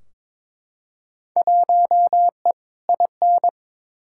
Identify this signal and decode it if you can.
Morse code, heard as a keyed tone, about 22 wpm, 700 Hz. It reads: 1E IN
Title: Field Day Exchanges - 22wpm